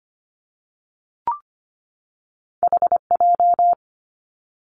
Morse code, heard as a keyed tone, about 25 wpm, 700 Hz. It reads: H J
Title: Two Challenging Letter-Number Combinations 25wpm